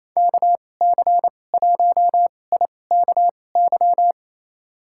Morse code, heard as a keyed tone, about 28 wpm, spaced as for 20 wpm, 700 Hz. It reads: KC1IKY
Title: USA Call Signs - 6 Characters - 2x Character Spacing - 28wpm